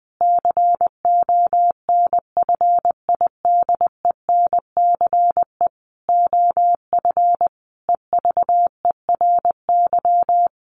Morse code, heard as a keyed tone, about 19 wpm, 700 Hz. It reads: CONFIDENCE OF EVERY
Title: Sentences from Top 1500 Words 20wpm